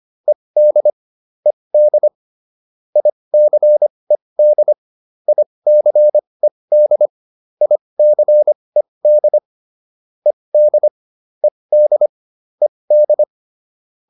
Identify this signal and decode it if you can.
Morse code, heard as a keyed tone, about 25 wpm, spaced as for 20 wpm, 600 Hz. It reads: ED ED ICED ICED ICED ED ED ED